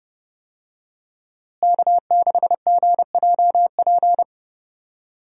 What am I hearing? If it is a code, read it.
Morse code, heard as a keyed tone, about 30 wpm, 700 Hz. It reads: K6GJP
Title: USA Call Signs - 5 Characters - Repeated 2x - 30wpm